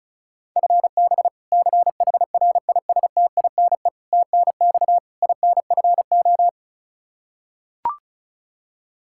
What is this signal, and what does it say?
Morse code, heard as a keyed tone, about 36 wpm, 700 Hz. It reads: FB CHRISTINE TNX INFO
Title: Practice QSO - Exchange 3 - More Info - 35wpm